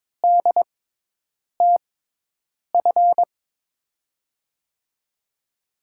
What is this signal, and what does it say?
Morse code, heard as a keyed tone, about 22 wpm, spaced as for 8 wpm, 700 Hz. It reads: DTF
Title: USA Call Signs - 5 Characters - 6x Character Spacing - 22wpm